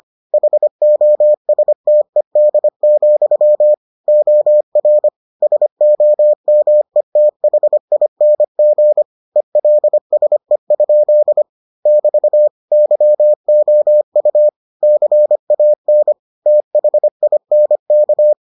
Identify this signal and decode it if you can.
Morse code, heard as a keyed tone, about 25 wpm, 600 Hz. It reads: HOSTED, OR SOMETHING ELSE? <BT> YOU CAN THINK